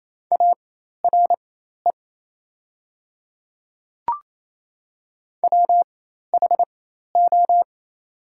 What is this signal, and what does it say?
Morse code, heard as a keyed tone, about 28 wpm, spaced as for 13 wpm, 690 Hz. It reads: ARE WHO
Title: 4x Character Spacing - Top 100 Words - 28wpm